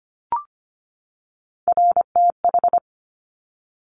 Morse code, heard as a keyed tone, about 25 wpm, 700 Hz. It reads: RTH
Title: Top Three Letter Combinations in Words 25wpm